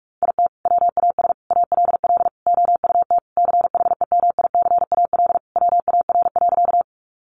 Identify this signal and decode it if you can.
Morse code, heard as a keyed tone, about 44 wpm, 700 Hz. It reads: IT WAS ALL OUT CHEMICAL WAR.
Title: Sentences from Top 1200 Words 45wpm